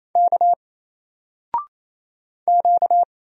Morse code, heard as a keyed tone, about 28 wpm, 700 Hz. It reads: K Q
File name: Single-Letter-Number-28wpm